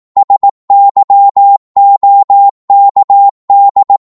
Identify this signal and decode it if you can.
Morse code, heard as a keyed tone, about 18 wpm, 800 Hz. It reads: SYOKD
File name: Morse / ltr-18wpm